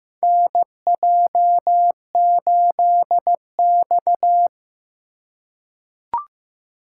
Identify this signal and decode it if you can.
Morse code, heard as a keyed tone, about 15 wpm, 700 Hz. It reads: NJ8X
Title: USA Call Signs - 4x Character Spacing - 15wpm